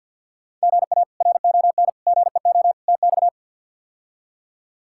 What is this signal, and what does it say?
Morse code, heard as a keyed tone, about 53 wpm, 700 Hz. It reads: GA RON GEO TX
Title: K1USN SST QSO Practice - 50wpm